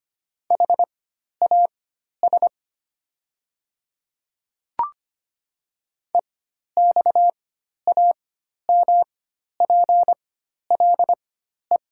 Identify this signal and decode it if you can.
Morse code, heard as a keyed tone, about 25 wpm, spaced as for 12 wpm, 700 Hz. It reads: HAS EXAMPLE
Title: 4x Character Spacing - Top 300 Words - 25wpm